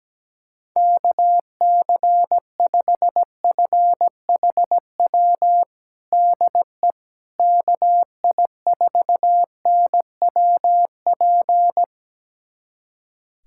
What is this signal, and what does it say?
Morse code, heard as a keyed tone, about 17 wpm, 700 Hz. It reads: KC5FHW DE KI4NWP
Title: USA Callsign DE Callsign 17wpm